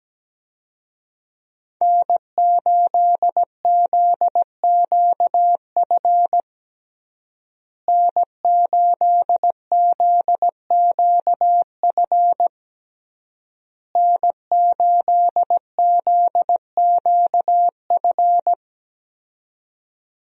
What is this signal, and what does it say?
Morse code, heard as a keyed tone, about 17 wpm, 700 Hz. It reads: N8ZQF N8ZQF N8ZQF
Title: USA Call Signs - 5 Characters - Repeated 3x - 17wpm